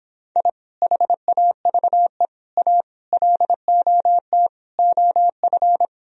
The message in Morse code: I HAVE A LOT OF

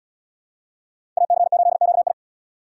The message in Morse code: AC8QI

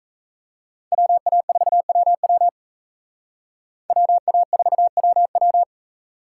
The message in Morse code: WA4WW WA4WW

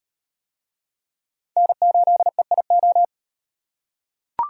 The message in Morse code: N8EIO

